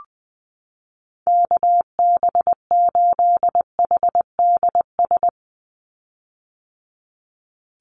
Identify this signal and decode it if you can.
Morse code, heard as a keyed tone, about 20 wpm, 700 Hz. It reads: KB8HDS
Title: USA Call Signs 20wpm